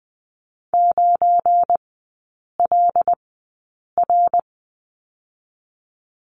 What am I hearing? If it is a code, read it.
Morse code, heard as a keyed tone, about 20 wpm, spaced as for 8 wpm, 700 Hz. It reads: 9LR
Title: 205 Lesson 33e - Letter j - Three Characters 20wpm